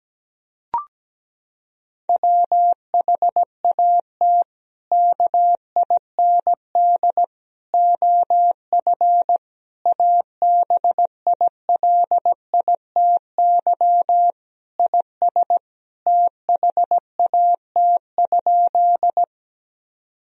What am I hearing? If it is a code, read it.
Morse code, heard as a keyed tone, about 17 wpm, 700 Hz. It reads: WHAT KIND OF ABILITY IS THAT?
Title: Sentences from Top 1000 Words - Review Part 3 - 17wpm